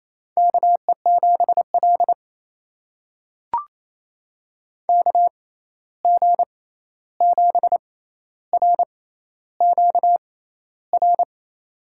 Morse code, heard as a keyed tone, about 28 wpm, spaced as for 13 wpm, 700 Hz. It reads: KE7L KG7RQR
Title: USA Call Signs - 6x Character Spacing - 28wpm